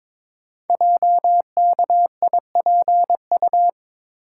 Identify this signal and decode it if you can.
Morse code, heard as a keyed tone, about 22 wpm, 690 Hz. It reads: JKIPU
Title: Code Groups - 5 Letters - 22wpm